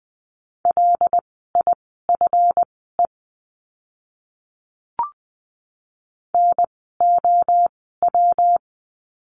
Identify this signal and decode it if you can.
Morse code, heard as a keyed tone, about 20 wpm, spaced as for 14 wpm, 700 Hz. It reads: LIFE NOW
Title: 230 Lesson 37f - Letter x - Words 20wpm